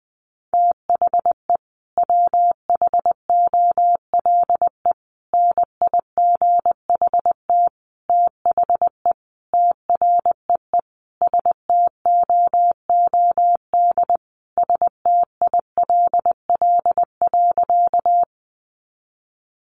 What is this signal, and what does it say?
Morse code, heard as a keyed tone, about 20 wpm, 700 Hz. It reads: THE WHOLE NIGHT THE TREE STOOD STILL.
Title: Sentences from Top 500 Words 20wpm